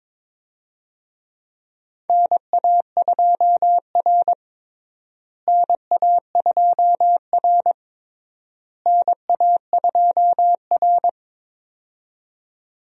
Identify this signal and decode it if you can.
Morse code, heard as a keyed tone, about 21 wpm, 700 Hz. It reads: NA2R NA2R NA2R
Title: USA Call Signs - 4 Characters - Repeated 3x - 22wpm